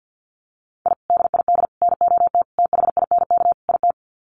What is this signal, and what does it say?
Morse code, heard as a keyed tone, about 50 wpm, 700 Hz. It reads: I DID NOT THINK IT
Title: Sets of 5 Words Encore2 50wpm